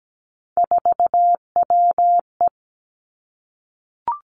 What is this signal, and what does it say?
Morse code, heard as a keyed tone, about 17 wpm, 700 Hz. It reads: 4WE
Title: Three Challenging Letter-Number Combinations 17wpm